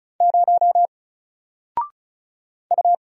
Morse code, heard as a keyed tone, about 36 wpm, 700 Hz. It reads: 0 U
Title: Single Letter-Number Alt 35wpm